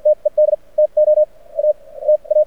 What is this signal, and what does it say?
Morse code, heard as a keyed tone, about 54 wpm, spaced as for 34 wpm, 610 Hz. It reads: TED TO ATA
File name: archived / CRSnet